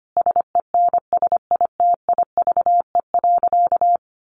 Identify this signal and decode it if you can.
Morse code, heard as a keyed tone, about 25 wpm, 700 Hz. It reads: SENSITIVE.
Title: Sentences from Top 1400 Words 25wpm